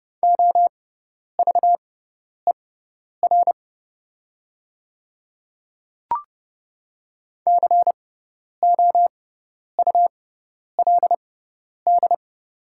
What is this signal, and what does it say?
Morse code, heard as a keyed tone, about 30 wpm, spaced as for 10 wpm, 700 Hz. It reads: OVER COULD
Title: 6x Character Spacing - Top 100 Words - 30wpm